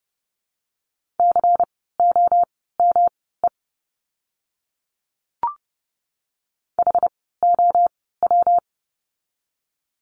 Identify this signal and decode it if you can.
Morse code, heard as a keyed tone, about 30 wpm, spaced as for 17 wpm, 700 Hz. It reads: COME HOW
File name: course / 30:10wpm / 237 Lesson 38f - Letter q - Words 30wpm